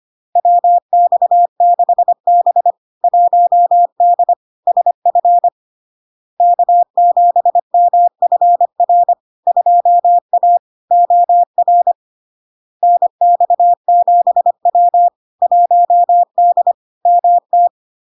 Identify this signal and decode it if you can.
Morse code, heard as a keyed tone, about 25 wpm, 700 Hz. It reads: WX6B 1D SF K7MFR 2A OR NX7W 1D MT
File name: Log-in-CW-25wpm-700-Hz